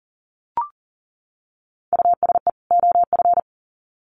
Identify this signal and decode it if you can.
Morse code, heard as a keyed tone, about 40 wpm, 700 Hz. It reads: USE OF
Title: Sets of 2 Words 40wpm